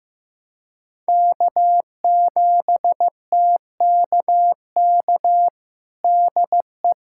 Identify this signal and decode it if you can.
Morse code, heard as a keyed tone, about 15 wpm, 700 Hz. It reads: K7TKK DE
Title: USA Callsign DE Callsign - 5 Characters - 15wpm